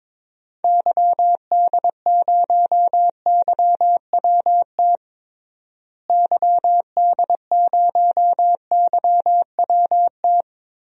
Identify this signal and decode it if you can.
Morse code, heard as a keyed tone, about 22 wpm, 700 Hz. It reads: YD0YWT YD0YWT
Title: DXCC-100 Call Signs - 6 Characters - Repeated 3x - 22wpm